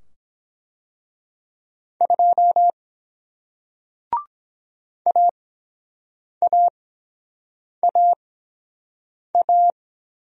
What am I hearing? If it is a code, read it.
Morse code, heard as a keyed tone, about 23 wpm, spaced as for 7 wpm, 700 Hz. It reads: 2 AAAA